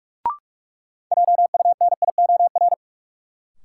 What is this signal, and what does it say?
Morse code, heard as a keyed tone, about 46 wpm, 700 Hz. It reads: JUNIOR
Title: Sentences from Top 1000 Words - Review Part 7 - New Word Drill - 45wpm